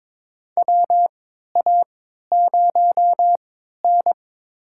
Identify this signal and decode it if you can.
Morse code, heard as a keyed tone, about 22 wpm, spaced as for 12 wpm, 700 Hz. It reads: WA0N
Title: USA Call Signs - 4 Characters - 3x Character Spacing - 22wpm